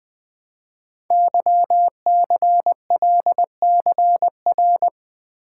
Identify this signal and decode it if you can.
Morse code, heard as a keyed tone, about 20 wpm, 700 Hz. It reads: YCLCR
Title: Code Groups - 5 Letters - 20wpm